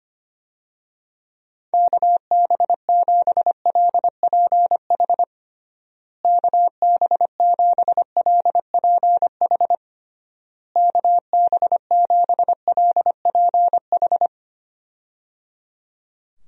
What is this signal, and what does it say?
Morse code, heard as a keyed tone, about 25 wpm, 700 Hz. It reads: KB7LPH KB7LPH KB7LPH
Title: USA Call Signs - 6 Characters - Repeated 3x - 25wpm